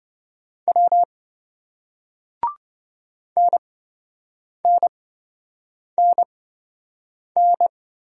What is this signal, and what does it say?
Morse code, heard as a keyed tone, about 28 wpm, spaced as for 7 wpm, 700 Hz. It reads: W NNNN